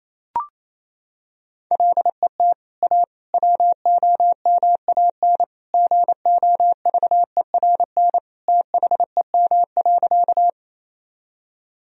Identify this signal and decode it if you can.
Morse code, heard as a keyed tone, about 28 wpm, 700 Hz. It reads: LET A WOMAN GOVERN THEM.
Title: Sentences from Top 1000 Words - Review Part 2 - 28wpm